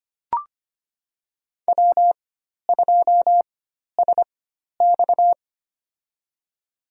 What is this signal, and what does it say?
Morse code, heard as a keyed tone, about 25 wpm, spaced as for 12 wpm, 700 Hz. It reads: W2SX
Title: USA Call Signs - 4 Characters - 4x Character Spacing - 25wpm